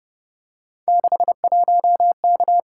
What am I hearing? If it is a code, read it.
Morse code, heard as a keyed tone, about 31 wpm, 700 Hz. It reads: 61K
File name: Speed-Racing - Three Letter-Number Combinations - 25wpm